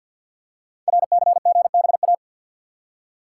Morse code, heard as a keyed tone, about 52 wpm, 700 Hz. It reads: RCGBA
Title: Code Groups - 5 Letters - 50wpm